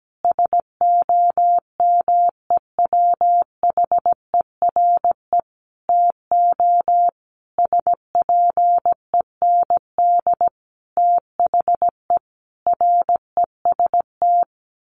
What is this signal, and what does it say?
Morse code, heard as a keyed tone, about 17 wpm, 700 Hz. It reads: SOMEWHERE TO SPEND THE REST
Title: Sentences from Top 1400 Words 17wpm